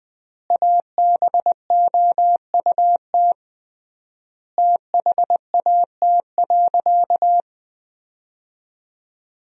Morse code, heard as a keyed tone, about 20 wpm, 700 Hz. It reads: ABOUT THAT.
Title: 3x Word-Spacing - Sentences from Top 200 Words - 20wpm